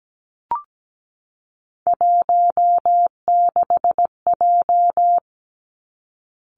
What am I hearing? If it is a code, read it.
Morse code, heard as a keyed tone, about 17 wpm, 700 Hz. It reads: 16J
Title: Three Challenging Letter-Number Combinations 17wpm